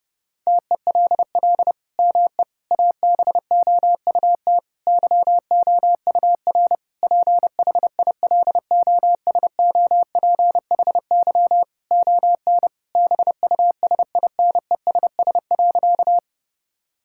Morse code, heard as a keyed tone, about 30 wpm, 700 Hz. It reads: TELL ME ABOUT YOUR PHILOSOPHY ON BUSINESS.
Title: Sentences from Top 1000 Words - Review Part 6 - 30wpm